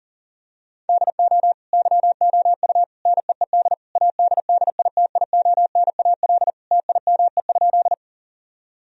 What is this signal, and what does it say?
Morse code, heard as a keyed tone, about 40 wpm, 700 Hz. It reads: DO YOU NEED ADDITIONAL TIME?
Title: Sentences from Top 1000 Words - Review Part 4 - 40wpm